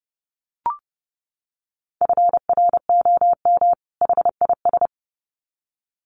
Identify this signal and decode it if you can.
Morse code, heard as a keyed tone, about 30 wpm, 700 Hz. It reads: FROM HIS
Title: Sets of 2 Words 30wpm